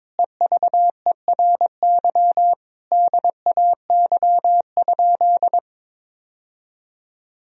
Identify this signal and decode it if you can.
Morse code, heard as a keyed tone, about 22 wpm, 700 Hz. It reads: EVERY DAY?
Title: Sentences from Top 1000 Words - Review Part 4 - 22wpm